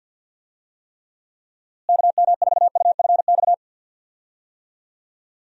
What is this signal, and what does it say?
Morse code, heard as a keyed tone, about 50 wpm, 700 Hz. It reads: KM4UFX
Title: USA Call Signs - 6 Characters - Repeated 2x - 50wpm